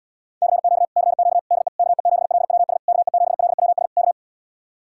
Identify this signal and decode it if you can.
Morse code, heard as a keyed tone, about 77 wpm, 700 Hz. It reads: CQ CQ DE KE7LWS KE7LWS K